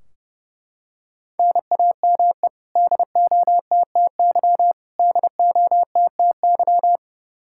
Morse code, heard as a keyed tone, about 30 wpm, 700 Hz. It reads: NAME DOTTY DOTTY
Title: Practice QSO - Exchange 1 - Beginning - 30wpm